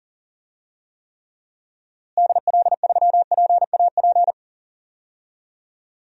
Morse code, heard as a keyed tone, about 40 wpm, 700 Hz. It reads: DL3PAP
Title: DXCC-100 Call Signs - 6 Characters - Repeated 2x - 40wpm